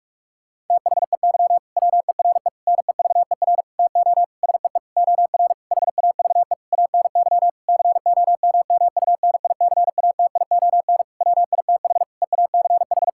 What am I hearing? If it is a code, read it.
Morse code, heard as a keyed tone, about 42 wpm, 700 Hz. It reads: THEY WERE NEVER TO SEE OR HAVE ANY COMMUNICATION WITH EACH
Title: Sentences from Top 1000 Words - Review Part 6 - 45wpm